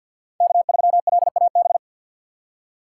Morse code, heard as a keyed tone, about 53 wpm, 700 Hz. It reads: K3LAB
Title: USA Call Signs - 5 Characters - 4x Character Spacing - 50wpm